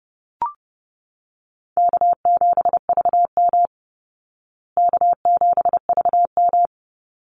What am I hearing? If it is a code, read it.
Morse code, heard as a keyed tone, about 30 wpm, 700 Hz. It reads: K7VM K7VM
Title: Call Signs Repeated 2x - 30wpm